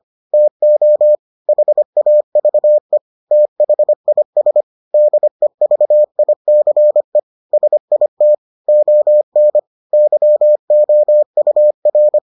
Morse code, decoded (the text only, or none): TO HAVE THIS DEVICE SIT ON YOUR